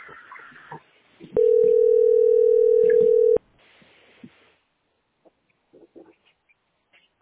Ringback tone (North America), 440 + 480 Hz.